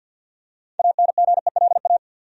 Morse code, heard as a keyed tone, about 49 wpm, 700 Hz. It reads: ANGELA
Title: USA Common Names - 50wpm